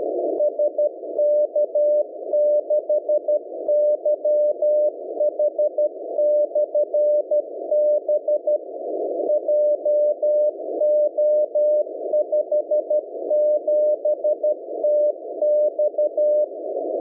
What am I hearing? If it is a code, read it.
Morse code, heard as a keyed tone, about 12 wpm, 600 Hz. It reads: SK6YH/B JO57TX